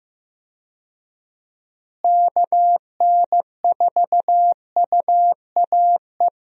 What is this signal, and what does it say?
Morse code, heard as a keyed tone, about 15 wpm, 700 Hz. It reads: KN4UAE
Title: USA Call Signs - 6 Characters - Repeated 3x - 15wpm